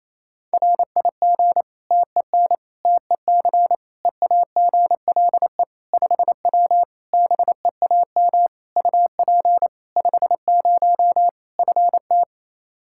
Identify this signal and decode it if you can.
Morse code, heard as a keyed tone, about 28 wpm, 700 Hz. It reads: RIG TEN TEC EAGLE 5W BEAM UP 50 FT
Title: Practice QSO - Exchange 2 - Info - 28wpm